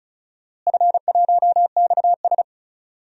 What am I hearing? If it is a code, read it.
Morse code, heard as a keyed tone, about 34 wpm, 700 Hz. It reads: F1XS
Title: DXCC-100 Call Signs - 4 Characters - Repeated 2x - 35wpm